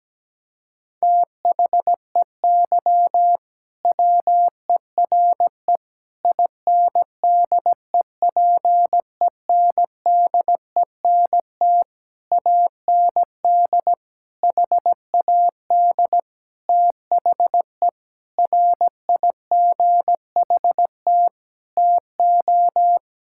Morse code, heard as a keyed tone, about 17 wpm, 700 Hz. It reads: THEY WERE INDEPENDENT AND HAD THE RIGHT TO
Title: Sentences from Top 1000 Words - Review Part 6 - 17wpm